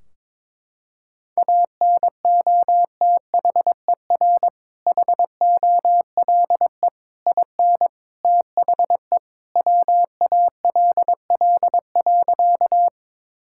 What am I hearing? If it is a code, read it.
Morse code, heard as a keyed tone, about 22 wpm, 700 Hz. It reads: ANOTHER HOLE IN THE WALL.
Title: Sentences from Top 1600 Words 22wpm